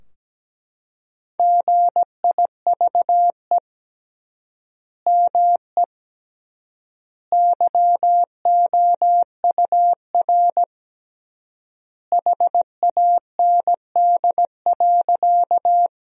Morse code, heard as a keyed tone, about 17 wpm, 700 Hz. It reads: GIVE ME YOUR HAND.
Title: 3x Word-Spacing - Sentences from Top 200 Words - 17wpm